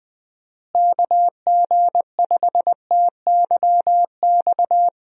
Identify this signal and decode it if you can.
Morse code, heard as a keyed tone, about 20 wpm, 700 Hz. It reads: KG5TYX
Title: USA Call Signs - 6 Characters - Repeated 3x - 20wpm